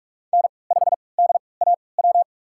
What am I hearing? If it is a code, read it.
Morse code, heard as a keyed tone, about 41 wpm, spaced as for 25 wpm, 700 Hz. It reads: N5DAW
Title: USA Call Signs - 5 Characters - 3x Character Spacing - 45wpm